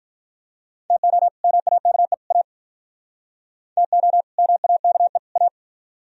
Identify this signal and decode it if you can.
Morse code, heard as a keyed tone, about 44 wpm, 700 Hz. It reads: TO MAKE A TO MAKE A
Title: Speed-Racing - Sets of 3 Words 35wpm